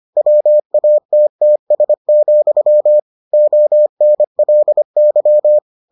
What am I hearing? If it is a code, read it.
Morse code, heard as a keyed tone, about 25 wpm, 600 Hz. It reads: WATTS, ONLY